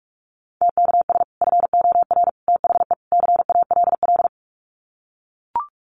Morse code, heard as a keyed tone, about 44 wpm, 700 Hz. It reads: TKS FOR THE CALL
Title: Common QSO Phrases 45wpm